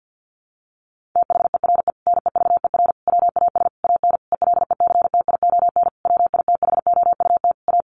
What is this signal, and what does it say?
Morse code, heard as a keyed tone, about 50 wpm, 700 Hz. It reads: THERE NEVER WAS AN ELECTION WITHOUT A
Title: Sentences from Top 1500 Words 50wpm